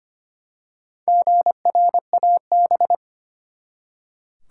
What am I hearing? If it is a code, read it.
Morse code, heard as a keyed tone, about 25 wpm, 700 Hz. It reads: GRAB
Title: Four Letter Words - 25wpm